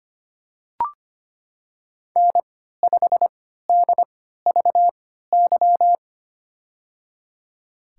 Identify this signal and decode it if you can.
Morse code, heard as a keyed tone, about 25 wpm, spaced as for 14 wpm, 700 Hz. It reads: N5DVY
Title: USA Call Signs - 3x Character Spacing - 25wpm